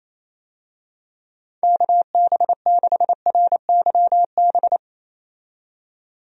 Morse code, heard as a keyed tone, about 28 wpm, 700 Hz. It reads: KB6RYB
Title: USA Call Signs - 6 Characters - 28wpm